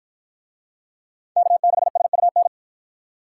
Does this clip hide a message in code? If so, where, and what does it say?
Yes: Morse code from 1.4 to 2.5 s, heard as a keyed tone, about 48 wpm, 700 Hz. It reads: K6SUN